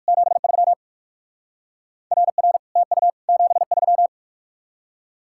Morse code, heard as a keyed tone, about 59 wpm, 700 Hz. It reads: 73 RR TU 73